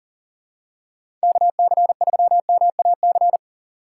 Morse code, heard as a keyed tone, about 40 wpm, 700 Hz. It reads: KC3MAC
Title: USA Call Signs - 6 Characters - 3x Character Spacing - 40wpm